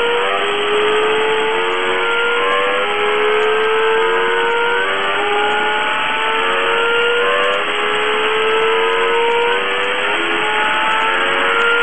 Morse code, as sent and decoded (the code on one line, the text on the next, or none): none